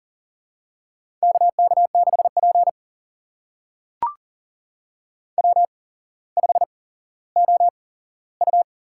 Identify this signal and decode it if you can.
Morse code, heard as a keyed tone, about 43 wpm, spaced as for 17 wpm, 700 Hz. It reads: KK6P W5OU
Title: USA Call Signs - 4 Characters - 8x Character Spacing - 40wpm